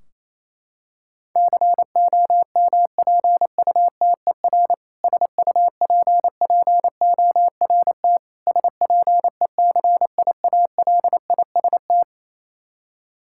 Morse code, heard as a keyed tone, about 28 wpm, 700 Hz. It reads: COMPUTER SUPPORT SPECIALIST
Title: K1USN SST QSO Practice 28wpm